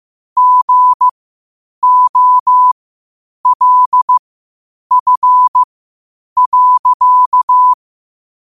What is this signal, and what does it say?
Morse code, heard as a keyed tone, about 15 wpm, spaced as for 8 wpm, 1000 Hz. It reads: GOLF.
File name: qotdfiles / 8wpm